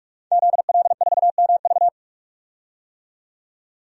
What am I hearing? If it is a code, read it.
Morse code, heard as a keyed tone, about 44 wpm, 700 Hz. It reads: ZL4MV
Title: DXCC-100 Call Signs - 5 Characters - 45wpm